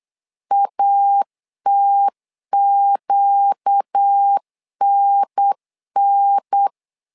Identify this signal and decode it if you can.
Morse code, heard as a keyed tone, about 8 wpm, 800 Hz. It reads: ATQNN
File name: LEZIONI / 4 Q 8wpm 2732